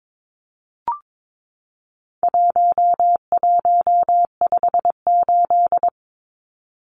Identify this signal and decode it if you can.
Morse code, heard as a keyed tone, about 22 wpm, 700 Hz. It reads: 1158
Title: Numbers - 4-Digit 22wpm